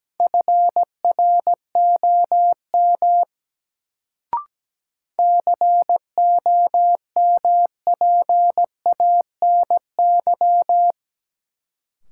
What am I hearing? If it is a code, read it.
Morse code, heard as a keyed tone, about 17 wpm, 700 Hz. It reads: FROM COMPANY
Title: Sets of 2 Words - New Word Drill - 17wpm